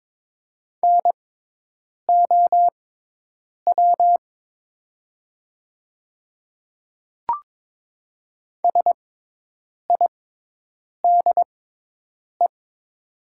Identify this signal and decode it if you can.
Morse code, heard as a keyed tone, about 22 wpm, spaced as for 8 wpm, 700 Hz. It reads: NOW SIDE
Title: 6x Character Spacing - Top 100 Words - 22wpm